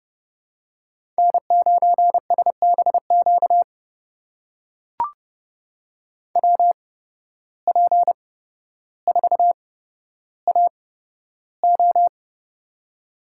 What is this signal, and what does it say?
Morse code, heard as a keyed tone, about 31 wpm, spaced as for 13 wpm, 700 Hz. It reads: N9SBQ WP4AO
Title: USA Call Signs - 5 Characters - 8x Character Spacing - 30wpm